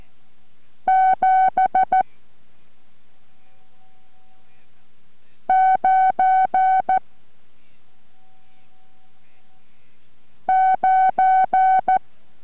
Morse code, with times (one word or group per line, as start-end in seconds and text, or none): 0.9-2.0 7
5.5-7.0 9
10.5-12.0 9